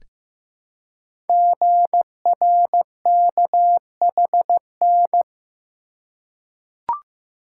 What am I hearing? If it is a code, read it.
Morse code, heard as a keyed tone, about 15 wpm, 700 Hz. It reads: GRKHN
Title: Code Groups - 5 Letter:Numbers - 15wpm